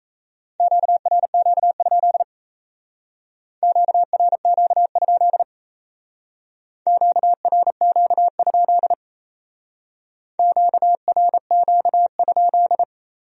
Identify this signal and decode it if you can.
Morse code, heard as a keyed tone, about 34 wpm, 700 Hz. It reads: QRQ? QRQ? QRQ? QRQ?
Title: Speed-Racing Q-Codes - 28wpm